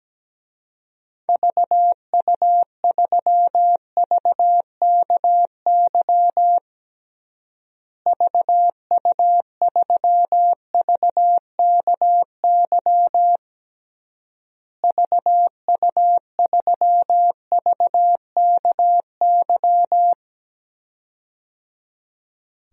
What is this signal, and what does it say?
Morse code, heard as a keyed tone, about 17 wpm, 700 Hz. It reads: VU3VKY VU3VKY VU3VKY
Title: DXCC-100 Call Signs - 6 Characters - Repeated 3x - 17wpm